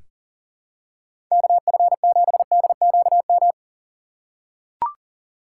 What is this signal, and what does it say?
Morse code, heard as a keyed tone, about 40 wpm, 700 Hz. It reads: KF7DQM
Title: USA Call Signs - 6 Characters - 2x Character Spacing - 40wpm